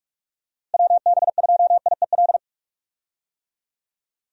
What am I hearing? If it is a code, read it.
Morse code, heard as a keyed tone, about 46 wpm, 700 Hz. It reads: WB2IEL
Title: USA Call Signs - 6 Characters - Repeated 2x - 45wpm